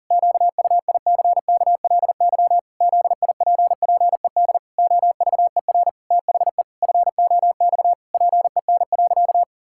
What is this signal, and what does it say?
Morse code, heard as a keyed tone, about 40 wpm, 700 Hz. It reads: QUICKLY ZIPPED OVER THE FOX PEN.
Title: Pangram Challenge v2 - 40wpm